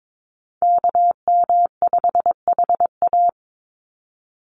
Morse code, heard as a keyed tone, about 22 wpm, 700 Hz. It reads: KM5HA
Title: USA Call Signs 22wpm